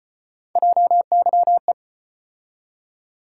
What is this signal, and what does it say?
Morse code, heard as a keyed tone, about 36 wpm, 700 Hz. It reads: JYE